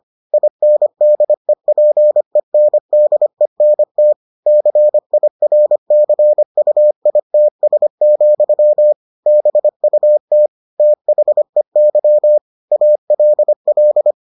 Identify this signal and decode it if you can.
Morse code, heard as a keyed tone, about 25 wpm, 600 Hz. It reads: INDEPENDENT CIRCUITS, BUT THEY ALL